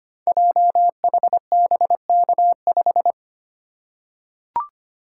Morse code, heard as a keyed tone, about 25 wpm, 700 Hz. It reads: JHBK5
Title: Code Groups - 5 Letter:Numbers - 25wpm